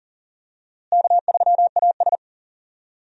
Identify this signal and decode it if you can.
Morse code, heard as a keyed tone, about 40 wpm, 700 Hz. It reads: K3AS
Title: USA Call Signs - 4 Characters - Repeated 2x - 40wpm